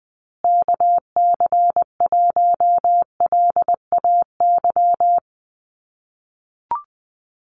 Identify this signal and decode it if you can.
Morse code, heard as a keyed tone, about 20 wpm, 700 Hz. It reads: KC1LAY
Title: USA Call Signs Repeated 2x 20wpm